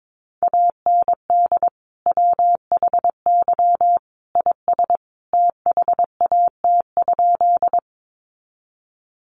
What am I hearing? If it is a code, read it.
Morse code, heard as a keyed tone, about 22 wpm, 700 Hz. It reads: AND WHY IS THAT?
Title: Sentences from Top 200 Words 22wpm